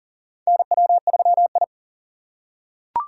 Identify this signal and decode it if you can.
Morse code, heard as a keyed tone, about 40 wpm, 700 Hz. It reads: NW3I
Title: USA Call Signs - 4 Characters - 4x Character Spacing - 40wpm